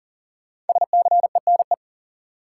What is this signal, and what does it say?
Morse code, heard as a keyed tone, about 40 wpm, 700 Hz. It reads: SCENE
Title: Sentences from Top 1100 Words - New Word Drill Alt - 40wpm